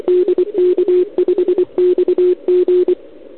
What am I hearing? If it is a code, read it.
Morse code, heard as a keyed tone, about 24 wpm, 360 Hz. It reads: DK5XG
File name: ARCHIV / OKDX2019 / OKOM40 KX3